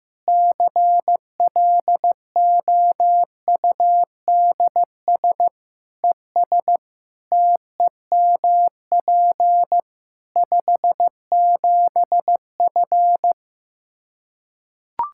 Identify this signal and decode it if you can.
Morse code, heard as a keyed tone, about 15 wpm, 700 Hz. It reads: CLOUDS ES TEMP 57F
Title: Practice QSO - Exchange 2 - Info - 15wpm